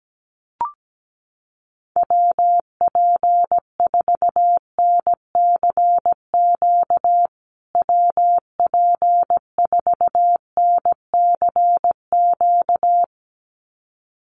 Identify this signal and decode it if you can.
Morse code, heard as a keyed tone, about 17 wpm, 700 Hz. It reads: WP4NCQ WP4NCQ
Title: USA Call Signs Repeated 2x 17wpm